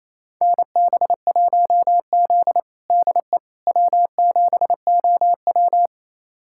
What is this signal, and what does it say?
Morse code, heard as a keyed tone, about 28 wpm, 700 Hz. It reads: NB1Z DE W7OW
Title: USA Callsign DE Callsign - 4 Characters - 28wpm